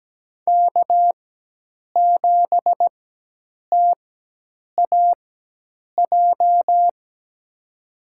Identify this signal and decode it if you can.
Morse code, heard as a keyed tone, about 17 wpm, spaced as for 8 wpm, 700 Hz. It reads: K7TAJ
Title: USA Call Signs - 4x Character Spacing - 17wpm